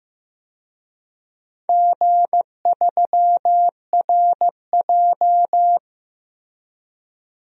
Morse code, heard as a keyed tone, about 15 wpm, 700 Hz. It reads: G3RJ
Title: DXCC-100 Call Signs - 4 Characters - Repeated 2x - 15wpm